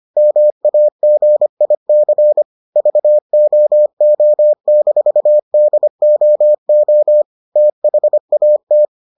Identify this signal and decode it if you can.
Morse code, heard as a keyed tone, about 25 wpm, 600 Hz. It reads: MAGIC VOO-DOO THAT